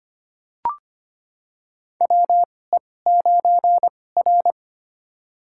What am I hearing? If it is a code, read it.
Morse code, heard as a keyed tone, about 25 wpm, spaced as for 18 wpm, 700 Hz. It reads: WE9R
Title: USA Call Signs - 4 Characters - 2x Character Spacing - 25wpm